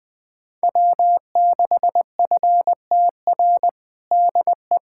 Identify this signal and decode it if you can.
Morse code, heard as a keyed tone, about 20 wpm, 700 Hz. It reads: W6FTR DE
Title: USA Callsign DE Callsign - 5 Characters - 20wpm